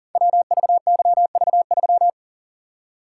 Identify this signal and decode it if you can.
Morse code, heard as a keyed tone, about 40 wpm, 700 Hz. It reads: WVYV3
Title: Code Groups - 5 Letter:Numbers - 40wpm